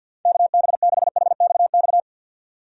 Morse code, heard as a keyed tone, about 52 wpm, 700 Hz. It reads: KB6HXX
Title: USA Call Signs - 6 Characters - 50wpm